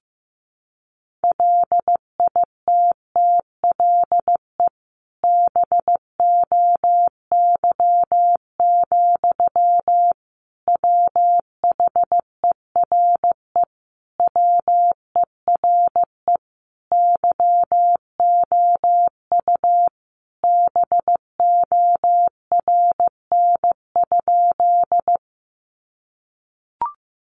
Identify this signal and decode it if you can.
Morse code, heard as a keyed tone, about 15 wpm, 700 Hz. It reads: LITTLE BOY, WHERE WERE YOU BORN?
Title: Sentences from Top 1000 Words 15wpm